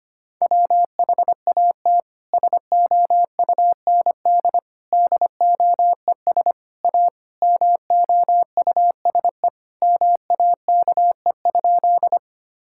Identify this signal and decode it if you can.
Morse code, heard as a keyed tone, about 25 wpm, 700 Hz. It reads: WHAT SOUND DOES A MOUSE MAKE?
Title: Sentences from Top 1000 Words - Review Part 5 - 25wpm